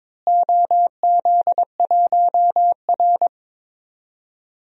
Morse code, heard as a keyed tone, about 22 wpm, 700 Hz. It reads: OZ1R
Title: DXCC-100 Call Signs - 4 Characters - Repeated 3x - 22wpm